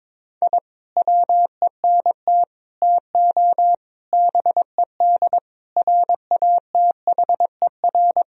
Morse code, heard as a keyed tone, about 21 wpm, 700 Hz. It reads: I WENT TO BED RATHER
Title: Sentences from Top 1000 Words - Review Part 4 - 22wpm